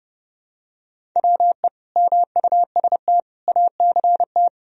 Morse code, heard as a keyed tone, about 30 wpm, 700 Hz. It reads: WE MUST ACT
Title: Sentences from Top 1000 Words - Review Part 1 - 30wpm